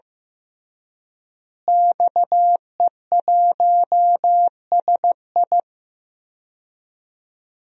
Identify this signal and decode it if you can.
Morse code, heard as a keyed tone, about 15 wpm, 700 Hz. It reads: XE1SI
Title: DXCC-100 Call Signs - 5 Characters - Repeated 3x - 15wpm